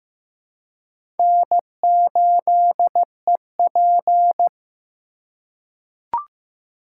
Morse code, heard as a keyed tone, about 15 wpm, 700 Hz. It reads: N8EP
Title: USA Call Signs - 2x Character Spacing - 15wpm